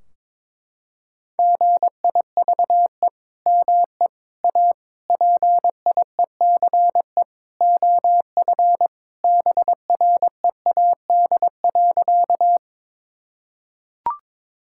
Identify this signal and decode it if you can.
Morse code, heard as a keyed tone, about 22 wpm, 700 Hz. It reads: GIVE ME A PIECE OF BREAD.
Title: Sentences from Top 1600 Words 22wpm